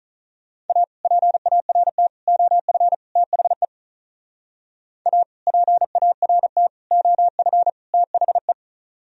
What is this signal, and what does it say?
Morse code, heard as a keyed tone, about 45 wpm, 700 Hz. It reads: A PART OF THE A PART OF THE